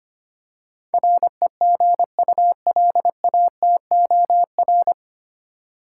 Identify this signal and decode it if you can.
Morse code, heard as a keyed tone, about 25 wpm, 700 Hz. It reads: REGULATOR
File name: practice / 25wpm / 9 Letter Words - First 3 Letters Given - 25wpm